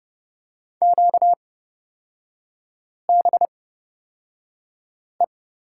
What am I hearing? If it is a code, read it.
Morse code, heard as a keyed tone, about 30 wpm, spaced as for 5 wpm, 700 Hz. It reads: QBE
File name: Lesson 38e - Letter q - Three Characters 30wpm